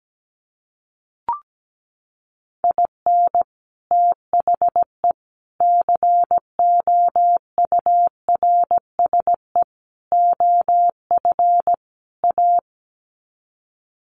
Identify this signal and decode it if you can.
Morse code, heard as a keyed tone, about 17 wpm, 700 Hz. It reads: IN THE COURSE OF A
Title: Sets of 5 Words Encore2 17wpm